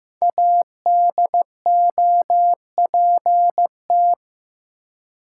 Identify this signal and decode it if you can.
Morse code, heard as a keyed tone, about 15 wpm, 700 Hz. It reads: ADOPT
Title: Five Letter Words - 15wpm